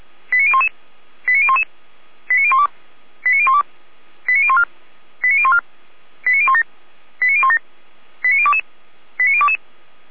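Five-tone selective call (ZVEI), heard as quick runs of five tones, tones of about 70 ms. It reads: 89011 89011 89012 89012 89014 89014 89017 89017 89022 89022
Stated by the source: ZVEI five-tone call received on an SDR, from the Feuerwehr Orlamünde (fire brigade) recording server.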